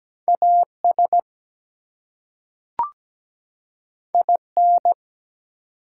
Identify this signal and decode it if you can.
Morse code, heard as a keyed tone, about 17 wpm, 700 Hz. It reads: AS IN
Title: Two Letter Words - 17wpm